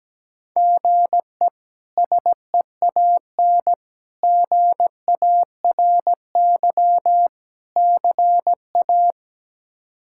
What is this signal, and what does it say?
Morse code, heard as a keyed tone, about 17 wpm, 700 Hz. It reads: GE SEAN GARY CA
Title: K1USN SST QSO Practice - 17wpm